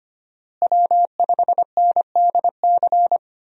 Morse code, heard as a keyed tone, about 25 wpm, 700 Hz. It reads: W5NDC
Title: USA Call Signs - 5 Characters - Repeated 3x - 25wpm